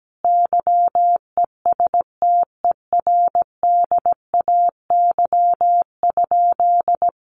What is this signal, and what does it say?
Morse code, heard as a keyed tone, about 17 wpm, 700 Hz. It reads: YESTERDAY?
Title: A Sound Of Thunder 17wpm